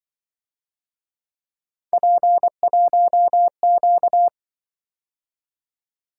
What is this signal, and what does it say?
Morse code, heard as a keyed tone, about 24 wpm, 700 Hz. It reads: P1Q
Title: Speed-Racing - Three Letter-Number Combinations - 20wpm